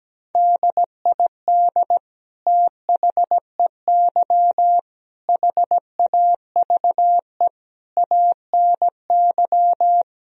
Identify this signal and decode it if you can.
Morse code, heard as a keyed tone, about 17 wpm, 700 Hz. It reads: DID THEY HAVE ANY
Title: Sentences from Top 1000 Words - Review Part 3 - 17wpm